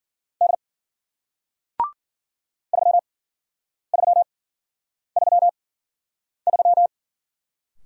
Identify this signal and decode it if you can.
Morse code, heard as a keyed tone, about 49 wpm, spaced as for 9 wpm, 700 Hz. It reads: D 3333
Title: Speed-Racing Single Letter-Number 40wpm